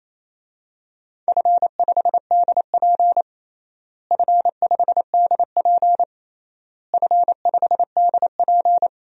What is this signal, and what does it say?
Morse code, heard as a keyed tone, about 28 wpm, 700 Hz. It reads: F5DP F5DP F5DP
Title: DXCC-100 Call Signs - 4 Characters - Repeated 3x - 28wpm